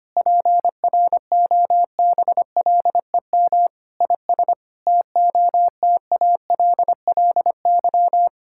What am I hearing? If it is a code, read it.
Morse code, heard as a keyed tone, about 25 wpm, 700 Hz. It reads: PROBLEM IS TOTALLY
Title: Sentences from Top 1600 Words 25wpm